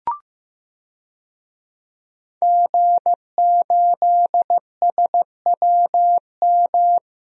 Morse code, heard as a keyed tone, about 16 wpm, 700 Hz. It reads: G8SWM
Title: DXCC-100 Call Signs - 5 Characters - Repeated 3x - 15wpm